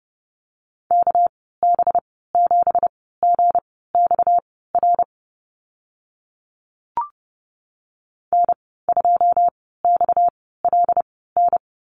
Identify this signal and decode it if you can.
Morse code, heard as a keyed tone, about 30 wpm, spaced as for 17 wpm, 700 Hz. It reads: KB7GXR N2XLN
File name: Lesson 37g - Letter x - Call Signs 30wpm